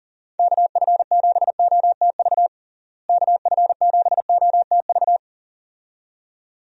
Morse code, heard as a keyed tone, about 40 wpm, 700 Hz. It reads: KF7OTV KF7OTV
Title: USA Call Signs - 6 Characters - Repeated 2x - 40wpm